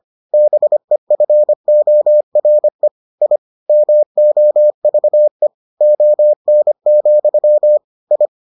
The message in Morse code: BEFORE I MOVE ON, I

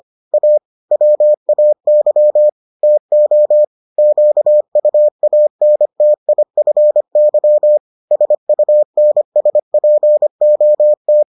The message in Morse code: A WAY TO QUANTIFY SUNSPOT

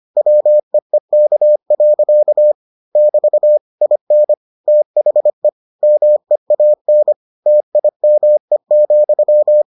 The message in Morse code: WEEK. <BT> IN THE MEAN TIME,

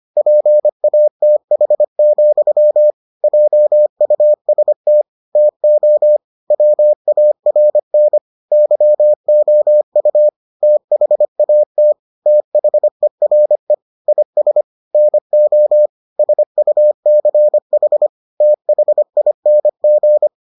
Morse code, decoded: PATH, JUST TO WARN YOU THAT THERE IS NO SUCH THING

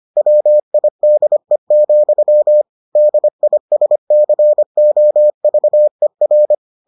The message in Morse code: WIDE, DISCOVER

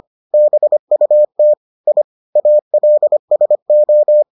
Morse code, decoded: BUT I ALSO

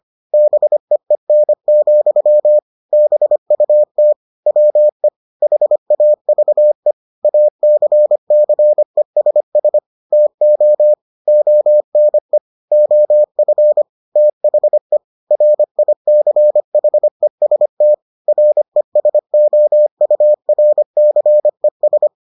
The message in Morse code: BEEN, BUT WE HAVE ACCESS TO ONE OF THE RICHEST RESOURCES

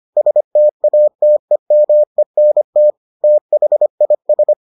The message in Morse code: STATEMENT THIS

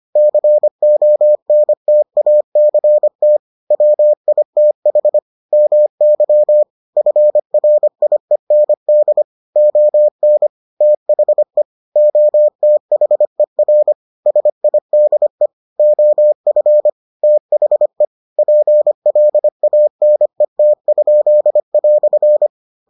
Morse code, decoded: CONTACT WITH MY FRIEND ON THE OTHER SIDE OF THE PLANET?"

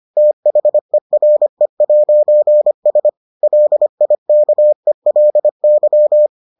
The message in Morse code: THERE'S LIKELY

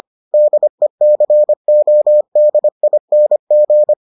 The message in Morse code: DECODING

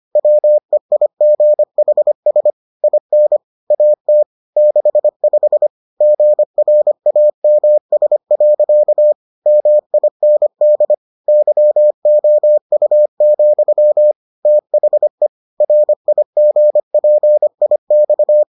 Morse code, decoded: WEIGHS IN AT 65 GRAMS. MIND YOU, THE RIGPIX

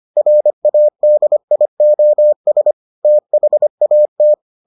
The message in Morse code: RADIOS THAT

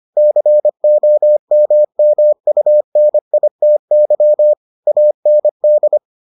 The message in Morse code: COMMUNITY AND

